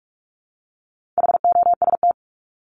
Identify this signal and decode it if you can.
Morse code, heard as a keyed tone, about 44 wpm, 700 Hz. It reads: HOST
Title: Top 1501-1750 Words 45wpm